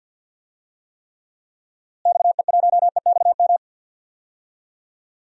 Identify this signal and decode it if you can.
Morse code, heard as a keyed tone, about 47 wpm, 700 Hz. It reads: XE1EXM
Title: DXCC-100 Call Signs - 6 Characters - Repeated 3x - 50wpm